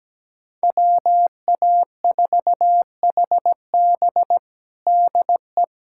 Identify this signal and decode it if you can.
Morse code, heard as a keyed tone, about 17 wpm, 700 Hz. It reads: WA4HB DE
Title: USA Callsign DE Callsign - 5 Characters - 17wpm